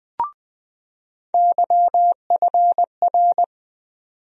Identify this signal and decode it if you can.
Morse code, heard as a keyed tone, about 20 wpm, 700 Hz. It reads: YFR
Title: USA Call Sign Postfixes - 3 Letters 20wpm